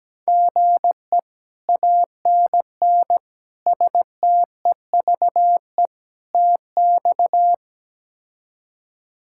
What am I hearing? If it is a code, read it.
Morse code, heard as a keyed tone, about 17 wpm, 700 Hz. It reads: GE ANN STEVE TX
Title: K1USN SST QSO Practice - 17wpm